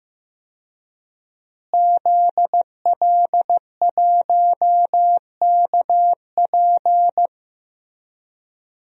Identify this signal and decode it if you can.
Morse code, heard as a keyed tone, about 15 wpm, 700 Hz. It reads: ZL1KP
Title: DXCC-100 Call Signs - 5 Characters - Repeated 3x - 15wpm